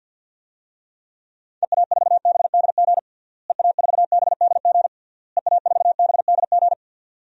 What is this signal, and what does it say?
Morse code, heard as a keyed tone, about 52 wpm, 700 Hz. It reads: EA4BDG EA4BDG EA4BDG
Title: DXCC-100 Call Signs - 6 Characters - Repeated 3x - 50wpm